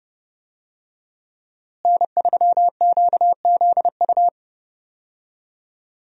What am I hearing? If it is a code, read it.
Morse code, heard as a keyed tone, about 30 wpm, 700 Hz. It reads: N3QZU
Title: USA Call Signs - 5 Characters - Repeated 3x - 30wpm